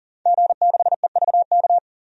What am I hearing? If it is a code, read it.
Morse code, heard as a keyed tone, about 40 wpm, 700 Hz. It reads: G6EVK
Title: DXCC-100 Call Signs - 5 Characters - 40wpm